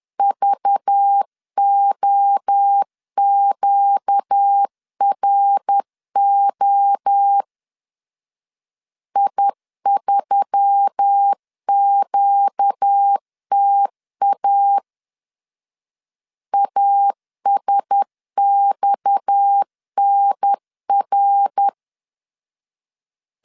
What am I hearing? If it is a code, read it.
Morse code, heard as a keyed tone, about 10 wpm, 800 Hz. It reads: VOQRO I3QTA ASXNR
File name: LEZIONI / 6 PKX0 / 6 PKX 10wpm 2426